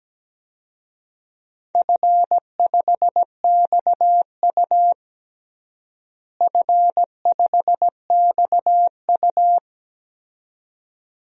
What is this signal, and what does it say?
Morse code, heard as a keyed tone, about 17 wpm, 700 Hz. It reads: F5XU F5XU
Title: DXCC-100 Call Signs - 4 Characters - Repeated 2x - 17wpm